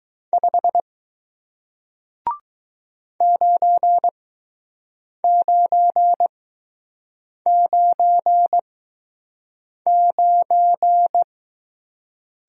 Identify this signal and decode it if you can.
Morse code, heard as a keyed tone, about 19 wpm, spaced as for 6 wpm, 700 Hz. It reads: 5 9999